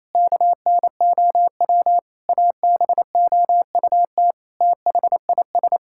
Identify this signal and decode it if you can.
Morse code, heard as a keyed tone, about 28 wpm, 700 Hz. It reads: KNOW ABOUT THIS
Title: Sentences from Top 1000 Words - Review Part 6 - 28wpm